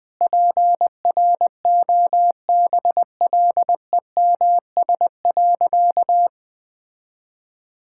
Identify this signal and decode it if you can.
Morse code, heard as a keyed tone, about 20 wpm, 700 Hz. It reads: PROBLEMS.
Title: All-Age Appropriate Jokes - 20wpm